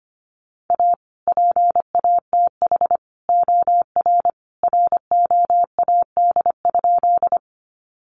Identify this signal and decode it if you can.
Morse code, heard as a keyed tone, about 25 wpm, 700 Hz. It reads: A PATH OR ROAD?
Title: Sentences from Top 1400 Words 25wpm